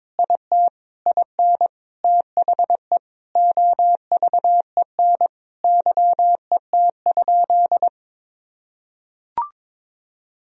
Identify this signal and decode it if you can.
Morse code, heard as a keyed tone, about 22 wpm, 700 Hz. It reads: IT IN THE OVEN YET?
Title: Sentences from Top 1000 Words - Review Part 4 - 22wpm